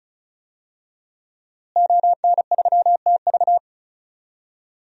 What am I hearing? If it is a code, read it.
Morse code, heard as a keyed tone, about 34 wpm, 700 Hz. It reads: ON3TV
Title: DXCC-100 Call Signs - 5 Characters - Repeated 3x - 35wpm